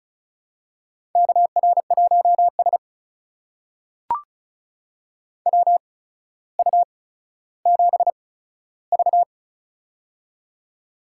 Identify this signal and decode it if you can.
Morse code, heard as a keyed tone, about 36 wpm, spaced as for 15 wpm, 700 Hz. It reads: KR1S WU7V
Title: USA Call Signs - 4 Characters - 8x Character Spacing - 35wpm